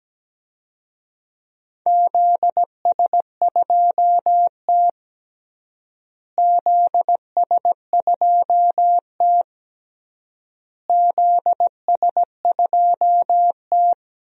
Morse code, heard as a keyed tone, about 17 wpm, 700 Hz. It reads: ZS2T ZS2T ZS2T
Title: DXCC-100 Call Signs - 4 Characters - Repeated 3x - 17wpm